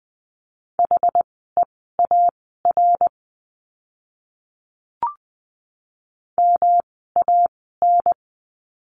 Morse code, heard as a keyed tone, about 20 wpm, spaced as for 15 wpm, 700 Hz. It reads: HEAR MAN
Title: 206 Lesson 33f - Letter j - Words 20wpm